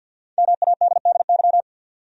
Morse code, heard as a keyed tone, about 52 wpm, 700 Hz. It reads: MADDY
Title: USA Common Names - Max 5 Letters - 50wpm